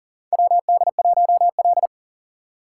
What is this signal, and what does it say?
Morse code, heard as a keyed tone, about 37 wpm, 700 Hz. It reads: WD1L
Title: USA Call Signs - 4 Characters - 4x Character Spacing - 40wpm